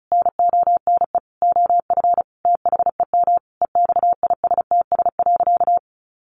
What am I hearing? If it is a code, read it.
Morse code, heard as a keyed tone, about 34 wpm, 700 Hz. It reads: NONE OF THEM EXISTS.
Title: A Sound Of Thunder 35wpm